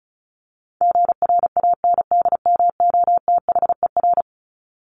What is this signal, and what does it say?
Morse code, heard as a keyed tone, about 34 wpm, 700 Hz. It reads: GRANDMOTHER
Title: Top 1001-1250 Words 35wpm